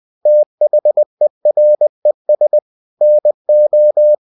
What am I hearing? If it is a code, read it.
Morse code, heard as a keyed tone, about 20 wpm, 600 Hz. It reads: THERES NO
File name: cw-books / winnie-the-pooh / winnie-20wpm-page-0050